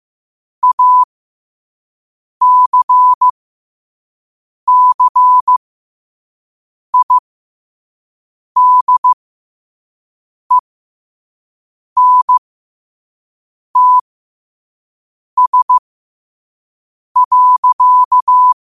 Morse code, heard as a keyed tone, about 15 wpm, spaced as for 5 wpm, 1000 Hz. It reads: ACCIDENTS.
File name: qotdfiles / 5wpm